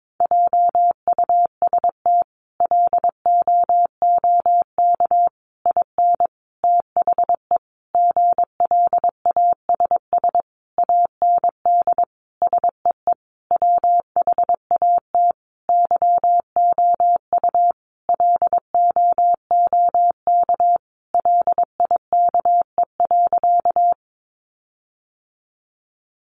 Morse code, heard as a keyed tone, about 22 wpm, 700 Hz. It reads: JUST LOOK IN THE GLASS AND SEE WHAT YOU LOOK LIKE.
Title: Sentences from Top 1300 Words 22wpm